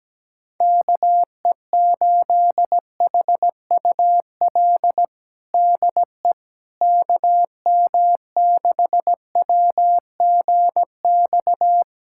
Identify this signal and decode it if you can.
Morse code, heard as a keyed tone, about 17 wpm, 700 Hz. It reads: KE8HUL DE KM6WGX
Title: USA Callsign DE Callsign 17wpm